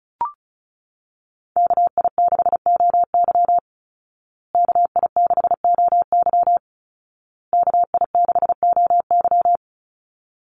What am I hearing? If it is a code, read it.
Morse code, heard as a keyed tone, about 34 wpm, 700 Hz. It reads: KI6OY KI6OY KI6OY
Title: Call Signs Repeated 2x - 35wpm